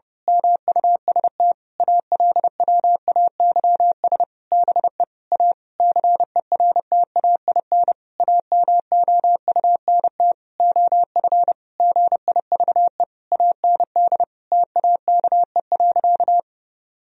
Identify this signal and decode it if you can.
Morse code, heard as a keyed tone, about 30 wpm, 700 Hz. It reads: MUST ALWAYS BE A CERTAIN AMOUNT OF GIVE AND TAKE.
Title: Sentences from Top 1000 Words - Review Part 4 - 30wpm